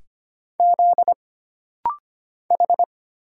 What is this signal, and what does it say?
Morse code, heard as a keyed tone, about 25 wpm, 700 Hz. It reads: Z H
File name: Single Character (Letter:Number:Punctuation:Prosign) - Rapid-Fire - 25wpm